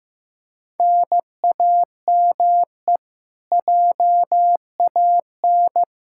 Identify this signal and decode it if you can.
Morse code, heard as a keyed tone, about 15 wpm, 700 Hz. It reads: NAME JAN
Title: Practice QSO - Exchange 1 - Beginning - 15wpm